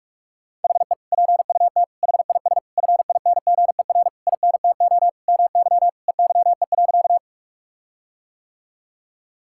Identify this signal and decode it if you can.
Morse code, heard as a keyed tone, about 46 wpm, 700 Hz. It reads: HE PUT HIS FINGER INTO MY EYE.
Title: Sentences from Top 1600 Words 45wpm